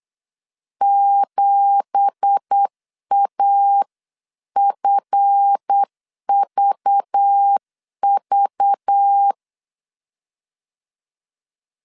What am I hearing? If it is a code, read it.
Morse code, heard as a keyed tone, about 8 wpm, 800 Hz. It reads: 7AFVV
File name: CW / DET / 1 9 1500C S A 8wpm 5637